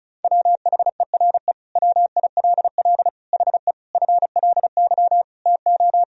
Morse code, heard as a keyed tone, about 36 wpm, 700 Hz. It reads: WHERE WILL HE FLY TO
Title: Sentences from Top 1000 Words - Review Part 2 - 35wpm